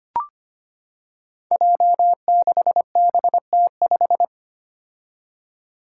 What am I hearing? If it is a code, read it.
Morse code, heard as a keyed tone, about 25 wpm, 700 Hz. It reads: J6BT5
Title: Five Challenging Letter-Number Combinations 25wpm